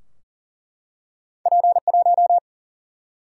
Morse code, heard as a keyed tone, about 40 wpm, 700 Hz. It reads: P1
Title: Single-Letter Single-Number 40wpm